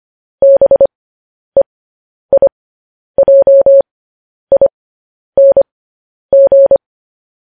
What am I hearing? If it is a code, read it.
Morse code, heard as a keyed tone, about 25 wpm, spaced as for 10 wpm, 550 Hz. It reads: BEIJING
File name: podcast / News headlines-1x-25x10